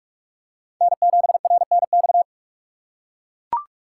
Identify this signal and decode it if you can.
Morse code, heard as a keyed tone, about 43 wpm, 690 Hz. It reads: N7RNX
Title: USA Call Signs - 5 Characters - 2x Character Spacing - 45wpm